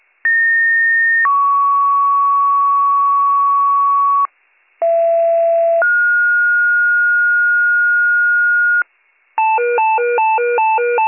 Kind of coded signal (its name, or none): two-tone page